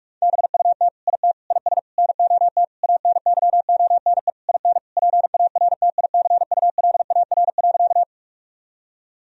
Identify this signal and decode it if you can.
Morse code, heard as a keyed tone, about 41 wpm, 700 Hz. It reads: BUT IT IS NOT ANYONE IN PARTICULAR.
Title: Sentences from Top 1000 Words - Review Part 4 - 45wpm